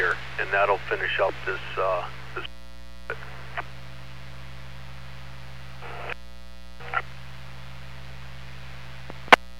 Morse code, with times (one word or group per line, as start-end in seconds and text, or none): none